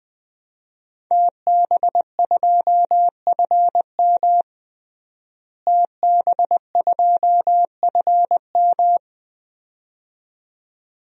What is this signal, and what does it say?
Morse code, heard as a keyed tone, about 20 wpm, 700 Hz. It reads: TB2FM TB2FM
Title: DXCC-100 Call Signs - 5 Characters - Repeated 2x - 20wpm